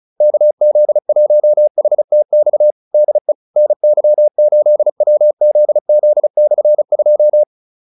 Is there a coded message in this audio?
Yes: Morse code from 0.2 to 7.4 s, heard as a keyed tone, about 34 wpm, 600 Hz. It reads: KZ1HTX DE NY8WZZ/2